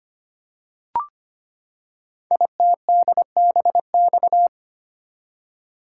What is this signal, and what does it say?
Morse code, heard as a keyed tone, about 25 wpm, 700 Hz. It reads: ITDBX
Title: Code Groups - 5 Letters - 25wpm